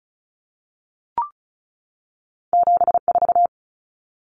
Morse code, heard as a keyed tone, about 36 wpm, 700 Hz. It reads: 74
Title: Common Cut Numbers - 2-Digit 35wpm